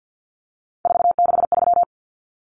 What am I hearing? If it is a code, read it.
Morse code, heard as a keyed tone, about 48 wpm, 700 Hz. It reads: 463
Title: Common Cut Numbers - 3-Digit 50wpm